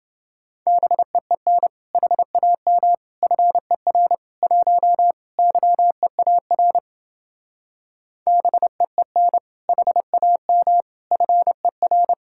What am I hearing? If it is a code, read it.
Morse code, heard as a keyed tone, about 30 wpm, 700 Hz. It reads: BEEN HAM FER 1 YEAR BEEN HAM FER